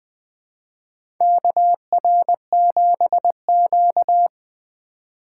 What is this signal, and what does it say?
Morse code, heard as a keyed tone, about 21 wpm, 700 Hz. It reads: KR7Q
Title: 2023 SOTA Chaser Call Signs - 20wpm